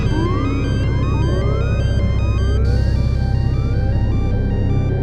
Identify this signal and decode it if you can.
There is no code.